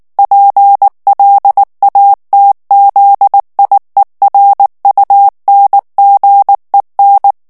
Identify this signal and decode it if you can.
Morse code, heard as a keyed tone, about 19 wpm, 800 Hz. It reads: PLATZIERUNGEN
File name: morse mp3 / ALLEMAND texte clair 2 vitesse 18 mots minute